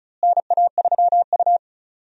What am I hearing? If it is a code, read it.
Morse code, heard as a keyed tone, about 36 wpm, 700 Hz. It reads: NA3U
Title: USA Call Signs - 4 Characters - Repeated 3x - 35wpm